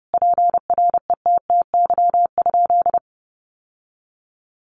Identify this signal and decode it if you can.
Morse code, heard as a keyed tone, about 30 wpm, 700 Hz. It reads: PRETTY?
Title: Sentences from Top 1000 Words 30wpm